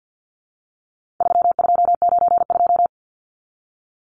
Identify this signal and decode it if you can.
Morse code, heard as a keyed tone, about 50 wpm, 700 Hz. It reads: 3292
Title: Numbers - 4-Digit 50wpm